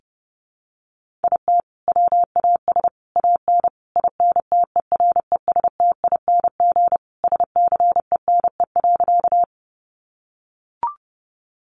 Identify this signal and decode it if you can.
Morse code, heard as a keyed tone, about 30 wpm, 700 Hz. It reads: IT WAS AN INTERESTING SCENE.
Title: Sentences from Top 1100 Words 30wpm